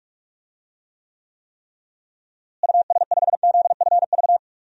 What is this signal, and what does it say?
Morse code, heard as a keyed tone, about 46 wpm, 700 Hz. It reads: US5ZFV
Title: DXCC-100 Call Signs - 6 Characters - Repeated 3x - 45wpm